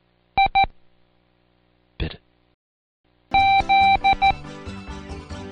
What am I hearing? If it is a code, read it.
Morse code, heard as a keyed tone, about 14 wpm, 780 Hz. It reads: I Z
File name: Code Quick / AudioDownld1 / Seg43-Day5-Intro Z 1